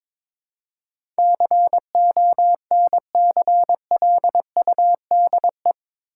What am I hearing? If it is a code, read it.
Morse code, heard as a keyed tone, about 22 wpm, 700 Hz. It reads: CONCLUDE
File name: practice / 22wpm / Eight Letter Words - 22wpm